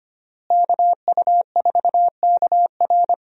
Morse code, heard as a keyed tone, about 25 wpm, 700 Hz. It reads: KU4KR
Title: USA Call Signs - 5 Characters - Repeated 3x - 25wpm